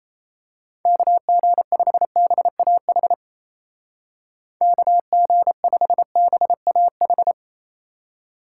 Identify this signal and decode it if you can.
Morse code, heard as a keyed tone, about 31 wpm, 700 Hz. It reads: KG5BAH KG5BAH
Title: Speed Racing - USA 2x3 Call Signs 28wpm